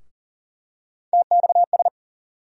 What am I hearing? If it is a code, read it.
Morse code, heard as a keyed tone, about 39 wpm, 700 Hz. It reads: TXS
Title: USA Call Sign Postfixes - 3 Letters 40wpm